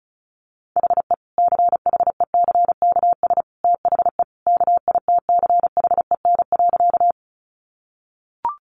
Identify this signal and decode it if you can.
Morse code, heard as a keyed tone, about 34 wpm, 700 Hz. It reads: HE CHECKS THE KITCHEN.
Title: Easy Phrases 35wpm